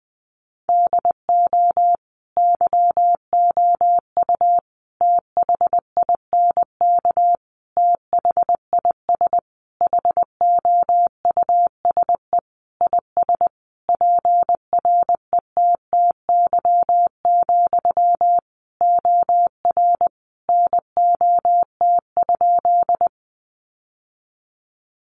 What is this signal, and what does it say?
Morse code, heard as a keyed tone, about 20 wpm, 700 Hz. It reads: DO YOU THINK THIS HOUSE IS PRETTY, OR NOT?
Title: Sentences from Top 1000 Words 20wpm